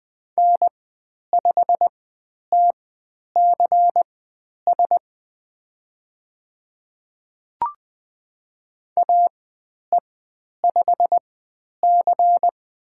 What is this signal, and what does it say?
Morse code, heard as a keyed tone, about 20 wpm, spaced as for 10 wpm, 700 Hz. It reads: N5TCS AE5C
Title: 102 Lesson 17g - Letter u - Call Signs 20wpm